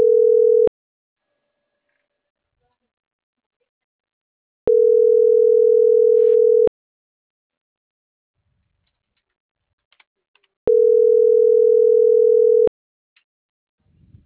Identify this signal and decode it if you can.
Ringback tone (North America), 440 + 480 Hz.